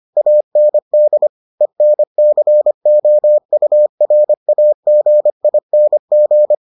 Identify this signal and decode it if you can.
Morse code, heard as a keyed tone, about 25 wpm, 600 Hz. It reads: AND ENCOURAGING